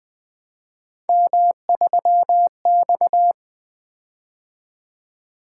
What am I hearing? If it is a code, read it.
Morse code, heard as a keyed tone, about 20 wpm, 700 Hz. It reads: M3X
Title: Three Letter-Number Combinations - 20wpm